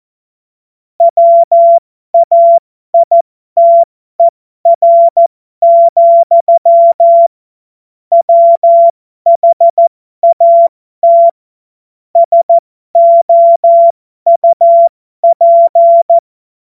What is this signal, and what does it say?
Morse code, heard as a keyed tone, about 14 wpm, spaced as for 12 wpm, 680 Hz. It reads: WAITER, WHAT SOUP
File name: Morsekurs / Witze